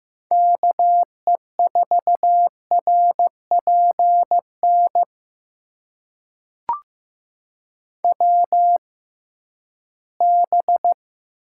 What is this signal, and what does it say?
Morse code, heard as a keyed tone, about 15 wpm, spaced as for 11 wpm, 700 Hz. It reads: KE4RPN WB
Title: USA Call Signs - 6 Characters - 6x Character Spacing - 15wpm